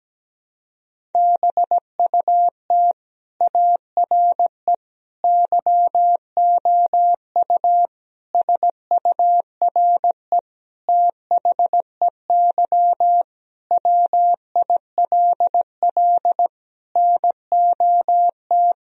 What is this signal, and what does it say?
Morse code, heard as a keyed tone, about 17 wpm, 700 Hz. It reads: BUT ARE YOU SURE THEY WILL NOT
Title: Sentences from Top 1000 Words - Review Part 1 - 17wpm